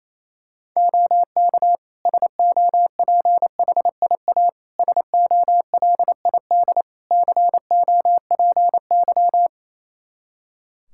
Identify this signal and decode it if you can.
Morse code, heard as a keyed tone, about 28 wpm, 700 Hz. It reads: OK SOPHIA SOLID COPY
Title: Practice QSO - Exchange 3 - More Info - 28wpm